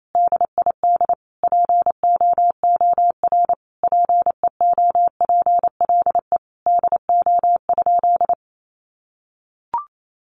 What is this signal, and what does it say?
Morse code, heard as a keyed tone, about 28 wpm, 700 Hz. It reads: DID POOR PEOPLE DO?
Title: Sentences from Top 1100 Words 28wpm